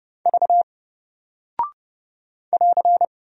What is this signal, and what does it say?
Morse code, heard as a keyed tone, about 30 wpm, 700 Hz. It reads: V <AR>
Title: Single Character (Letter:Number:Punctuation:Prosign) 30wpm